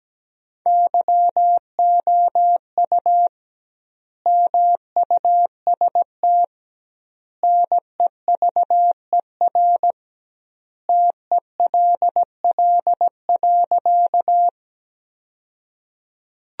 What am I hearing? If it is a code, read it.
Morse code, heard as a keyed tone, about 17 wpm, 700 Hz. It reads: YOU MUST NEVER TELL.
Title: 2x Word-Spacing - Sentences from Top 300 Words - 17wpm